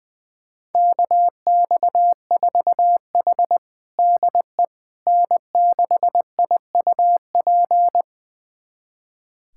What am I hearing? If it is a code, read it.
Morse code, heard as a keyed tone, about 20 wpm, 700 Hz. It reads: KX4H DE N6IUP
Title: USA Callsign DE Callsign 20wpm